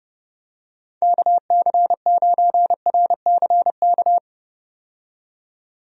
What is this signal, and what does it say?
Morse code, heard as a keyed tone, about 30 wpm, 700 Hz. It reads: KC9RCK
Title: USA Call Signs - 6 Characters - 30wpm